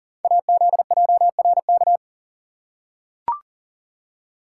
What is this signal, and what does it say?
Morse code, heard as a keyed tone, about 42 wpm, 700 Hz. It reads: AZJRK
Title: Code Groups - 5 Letter:Numbers - 40wpm